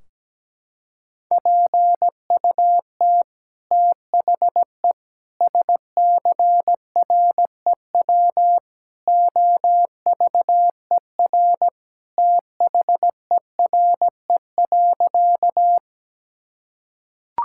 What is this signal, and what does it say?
Morse code, heard as a keyed tone, about 17 wpm, 700 Hz. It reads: PUT THE SCREW OVER THERE.
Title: Sentences from Top 1600 Words 17wpm